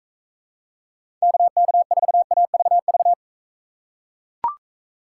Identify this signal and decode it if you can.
Morse code, heard as a keyed tone, about 42 wpm, 700 Hz. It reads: KK4AVV